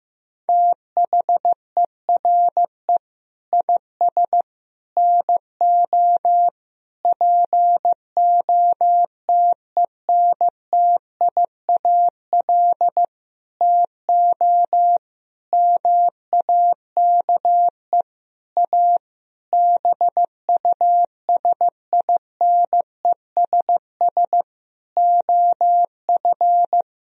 Morse code, heard as a keyed tone, about 15 wpm, 700 Hz. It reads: THERE IS NO POTENTIAL TO MAKE A BUSINESS OF
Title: Sentences from Top 1000 Words - Review Part 4 - 15wpm